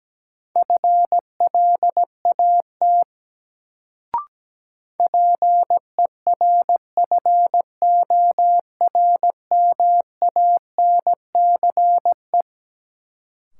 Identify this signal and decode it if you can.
Morse code, heard as a keyed tone, about 17 wpm, 700 Hz. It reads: FLAT PERFORMANCE
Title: Sentences from Top 1000 Words - New Word Drill - 17wpm